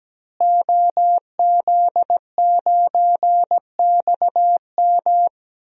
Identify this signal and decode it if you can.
Morse code, heard as a keyed tone, about 17 wpm, 700 Hz. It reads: OZ9XM
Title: DXCC-100 Call Signs - 5 Characters - Repeated 2x - 17wpm